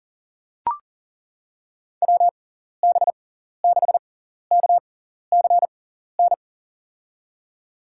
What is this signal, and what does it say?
Morse code, heard as a keyed tone, about 40 wpm, spaced as for 14 wpm, 700 Hz. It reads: WB6KCN
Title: USA Call Signs - 6 Characters - 6x Character Spacing - 40wpm